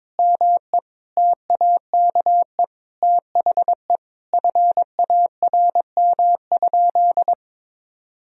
Morse code, heard as a keyed tone, about 22 wpm, 700 Hz. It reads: ME TAKE THE FARM?
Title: Sentences from Top 1000 Words - Review Part 2 - 22wpm